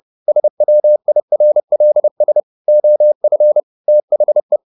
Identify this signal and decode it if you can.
Morse code, heard as a keyed tone, about 30 wpm, 600 Hz. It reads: SWIRLS OF THE
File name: the old man and the sea / the old man and the sea 1 30wpm